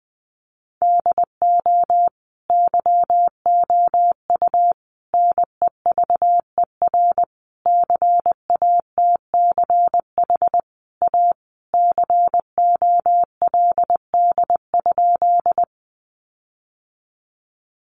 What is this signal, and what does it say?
Morse code, heard as a keyed tone, about 20 wpm, 700 Hz. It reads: DO YOU NEVER CATCH A COLD?
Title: Sentences from Top 1500 Words 20wpm